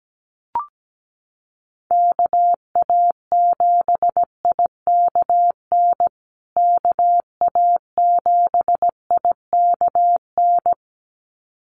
Morse code, heard as a keyed tone, about 17 wpm, 700 Hz. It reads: KA7IKN KA7IKN
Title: USA Call Signs Repeated 2x 17wpm